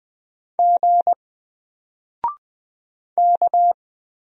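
Morse code, heard as a keyed tone, about 20 wpm, 700 Hz. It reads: G K
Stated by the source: Single letter and number Morse drill at 20 wpm, from N4LCD.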